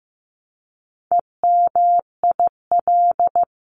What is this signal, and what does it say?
Morse code, heard as a keyed tone, about 15 wpm, 700 Hz. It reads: EMIL
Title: CWT Exchange 15wpm